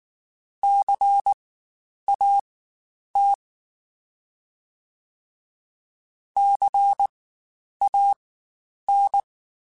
Morse code, heard as a keyed tone, about 19 wpm, spaced as for 9 wpm, 800 Hz. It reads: CAT CAN
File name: Spaced Alphas / Spaced Alphas mp3s / Cs sp @18